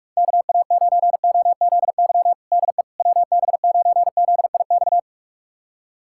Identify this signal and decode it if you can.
Morse code, heard as a keyed tone, about 41 wpm, 700 Hz. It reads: KA9OZY DE WB9ZIX
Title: USA Callsign DE Callsign - 6 Characters - 45wpm